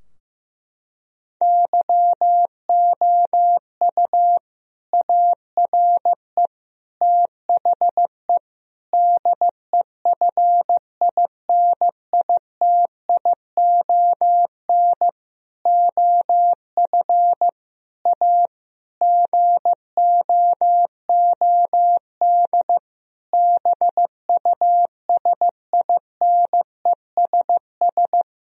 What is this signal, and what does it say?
Morse code, heard as a keyed tone, about 14 wpm, 700 Hz. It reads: YOU ARE THE DEFINITION OF A GOOD BUSINESS
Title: Sentences from Top 1000 Words - Review Part 4 - 15wpm